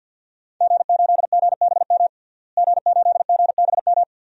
Morse code, heard as a keyed tone, about 51 wpm, 700 Hz. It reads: G8GBM G8GBM
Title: DXCC-100 Call Signs - 5 Characters - Repeated 2x - 50wpm